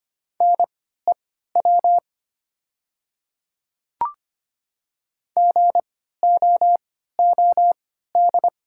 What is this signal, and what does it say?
Morse code, heard as a keyed tone, about 25 wpm, spaced as for 14 wpm, 700 Hz. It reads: NEW GOOD
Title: 3x Character Spacing - Top 300 Words - 25wpm